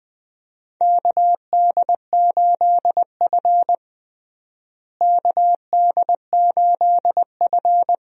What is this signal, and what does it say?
Morse code, heard as a keyed tone, about 21 wpm, 700 Hz. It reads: KD8F KD8F
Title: USA Call Signs - 4 Characters - Repeated 2x - 20wpm